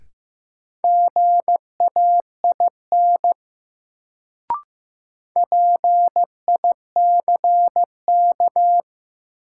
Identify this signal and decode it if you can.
Morse code, heard as a keyed tone, about 15 wpm, 700 Hz. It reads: GAIN PICK
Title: Sentences from Top 1000 Words - New Word Drill - 15wpm